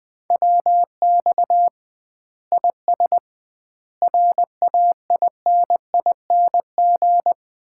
Morse code, heard as a keyed tone, about 20 wpm, 700 Hz. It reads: WX IS RAINING
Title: Common QSO Phrases - 2x Word-Spacing - 20wpm